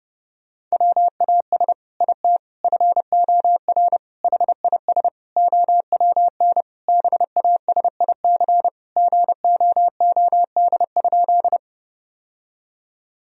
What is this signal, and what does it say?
Morse code, heard as a keyed tone, about 32 wpm, 700 Hz. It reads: WAS IT FOR HIS OWN BASIC GOOD?
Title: Sentences from Top 1000 Words - Review Part 3 - 30wpm